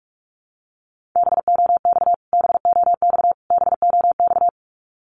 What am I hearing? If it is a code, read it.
Morse code, heard as a keyed tone, about 44 wpm, 700 Hz. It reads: BOX BOX BOX
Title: Top 500 Words Repeated 45wpm